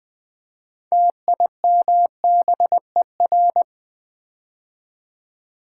Morse code, heard as a keyed tone, about 20 wpm, 700 Hz. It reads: TIMBER
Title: Build A Word - 3 Letters At A Time - 6-Letter Words - 20wpm